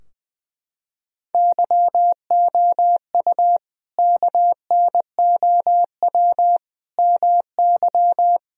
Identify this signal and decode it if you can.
Morse code, heard as a keyed tone, about 20 wpm, 700 Hz. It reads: YOU KNOW MY
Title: Sentences from Top 1000 Words - Review Part 3 - 20wpm